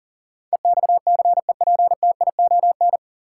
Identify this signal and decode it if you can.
Morse code, heard as a keyed tone, about 37 wpm, 700 Hz. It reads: EXCEPTION
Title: Sets of 4 Words - New Word Drill - 40wpm